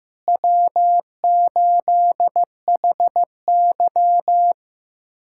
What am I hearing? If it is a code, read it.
Morse code, heard as a keyed tone, about 16 wpm, 700 Hz. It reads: W8HY
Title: USA Callsign DE Callsign - 4 Characters - 15wpm